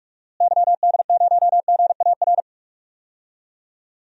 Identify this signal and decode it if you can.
Morse code, heard as a keyed tone, about 46 wpm, 700 Hz. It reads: YD0GAR
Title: DXCC-100 Call Signs - 6 Characters - Repeated 2x - 45wpm